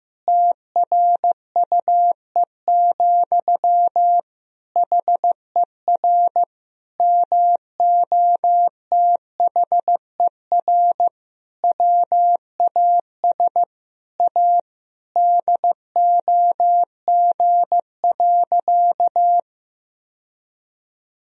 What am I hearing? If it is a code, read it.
Morse code, heard as a keyed tone, about 15 wpm, 700 Hz. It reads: TRUE, HER MOTHER WAS A DOG.
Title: Sentences from Top 1000 Words - Review Part 2 - 15wpm